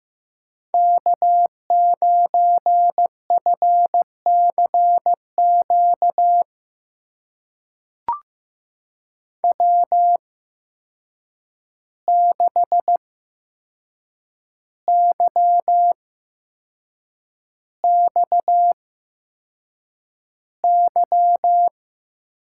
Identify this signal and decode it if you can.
Morse code, heard as a keyed tone, about 15 wpm, spaced as for 6 wpm, 700 Hz. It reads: K9FCQ W6YXY
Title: USA Call Signs - 5 Characters - 8x Character Spacing - 15wpm